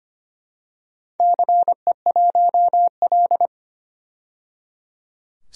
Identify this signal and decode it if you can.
Morse code, heard as a keyed tone, about 25 wpm, 700 Hz. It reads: CE1L
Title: DXCC-100 Call Signs - 4 Characters - 25wpm